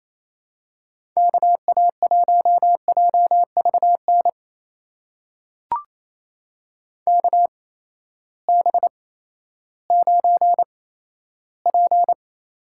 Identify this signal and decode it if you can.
Morse code, heard as a keyed tone, about 28 wpm, spaced as for 14 wpm, 700 Hz. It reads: KA1JVN KB9P
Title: USA Call Signs - 6 Characters - 8x Character Spacing - 28wpm